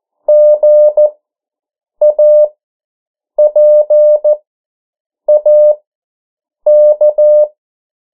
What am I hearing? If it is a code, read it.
Morse code, heard as a keyed tone, about 14 wpm, spaced as for 7 wpm, 600 Hz. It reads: GAPAK
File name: morse mp3 / 09 urok 8